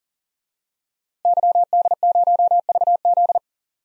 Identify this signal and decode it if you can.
Morse code, heard as a keyed tone, about 40 wpm, 700 Hz. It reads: YD0VZ
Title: DXCC-100 Call Signs - 5 Characters - Repeated 2x - 40wpm